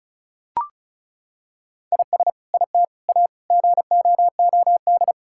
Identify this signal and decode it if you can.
Morse code, heard as a keyed tone, about 36 wpm, 700 Hz. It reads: IS IT A GOOD
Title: Sentences from Top 1000 Words - Review Part 5 - 35wpm